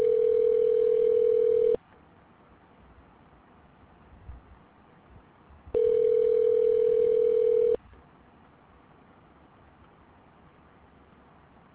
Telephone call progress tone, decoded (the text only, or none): ringback tone